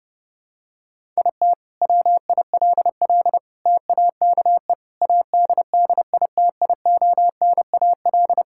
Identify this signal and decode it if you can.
Morse code, heard as a keyed tone, about 32 wpm, 700 Hz. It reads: IT WILL TAKE ADDITIONAL
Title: Sentences from Top 1000 Words - Review Part 5 - 30wpm